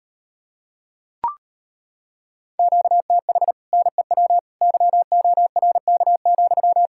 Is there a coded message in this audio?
Yes: Morse code from 2.6 to 6.9 s, heard as a keyed tone, about 33 wpm, 700 Hz. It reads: QTH NEW YORK,